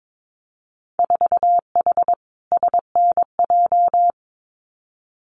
Morse code, heard as a keyed tone, about 22 wpm, 700 Hz. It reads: 4H SNJ
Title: Winter Field Day Exchanges Encore 22wpm